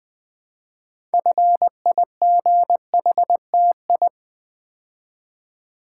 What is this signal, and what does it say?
Morse code, heard as a keyed tone, about 20 wpm, 700 Hz. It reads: FIGHTI
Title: Build A Word - 3 Letters At A Time - 8-Letter Words - 20wpm